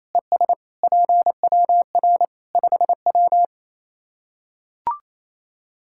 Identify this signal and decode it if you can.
Morse code, heard as a keyed tone, about 28 wpm, 700 Hz. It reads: ES PWR 5W
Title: Practice QSO - Exchange 2 - Info - 28wpm